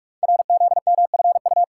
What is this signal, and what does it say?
Morse code, heard as a keyed tone, about 44 wpm, 700 Hz. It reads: RZMFU
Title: Code Groups - 5 Letter:Numbers - 45wpm